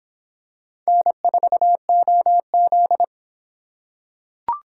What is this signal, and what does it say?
Morse code, heard as a keyed tone, about 24 wpm, 700 Hz. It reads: N4OZ